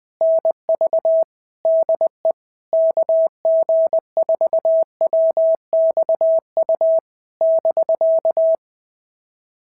Morse code, heard as a keyed tone, about 20 wpm, 650 Hz. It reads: NV DE KG4WXU <BK>
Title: POTA QSO Practice - 20wpm